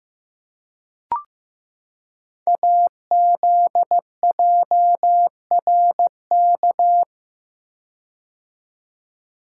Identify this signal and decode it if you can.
Morse code, heard as a keyed tone, about 15 wpm, 700 Hz. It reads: AZJRK
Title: Code Groups - 5 Letter:Numbers - 15wpm